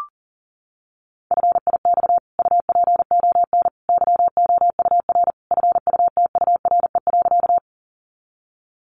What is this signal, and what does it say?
Morse code, heard as a keyed tone, about 37 wpm, 700 Hz. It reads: FIX UPON YOUR FUTURE.
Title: Sentences from Top 1400 Words 40wpm